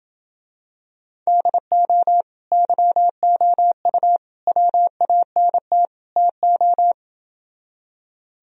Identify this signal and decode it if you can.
Morse code, heard as a keyed tone, about 27 wpm, 700 Hz. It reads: DO YOU WANT TO